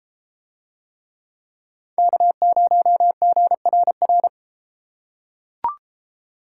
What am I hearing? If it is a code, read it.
Morse code, heard as a keyed tone, about 32 wpm, 700 Hz. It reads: K0GRR